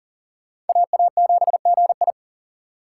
Morse code, heard as a keyed tone, about 52 wpm, 700 Hz. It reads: AA7GI